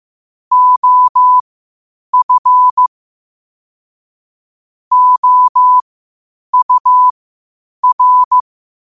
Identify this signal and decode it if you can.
Morse code, heard as a keyed tone, about 15 wpm, spaced as for 8 wpm, 1000 Hz. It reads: OF OUR
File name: 8wpm